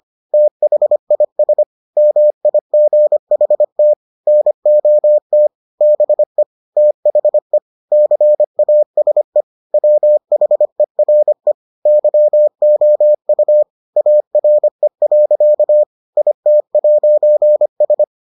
Morse code, heard as a keyed tone, about 25 wpm, 600 Hz. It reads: THIS MIGHT NOT BE THE CASE WHERE YOU ARE. IT'S